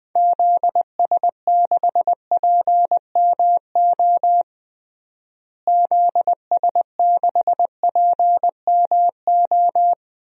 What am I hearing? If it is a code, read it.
Morse code, heard as a keyed tone, about 21 wpm, 700 Hz. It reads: ZS6PMO ZS6PMO
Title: DXCC-100 Call Signs - 6 Characters - Repeated 2x - 20wpm